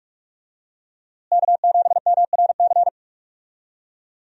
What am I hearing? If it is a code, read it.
Morse code, heard as a keyed tone, about 42 wpm, 700 Hz. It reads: K7MRC
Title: USA Call Signs - 5 Characters - Repeated 2x - 45wpm